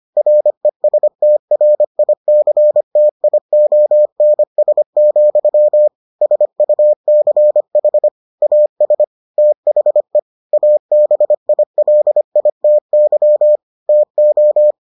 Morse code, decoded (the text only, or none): RESTRICTIONS, SUCH AS THE ABILITY TO